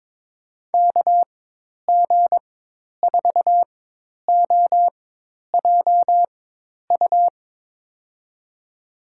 Morse code, heard as a keyed tone, about 22 wpm, spaced as for 10 wpm, 700 Hz. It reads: KG4OJU
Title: USA Call Signs - 6 Characters - 4x Character Spacing - 22wpm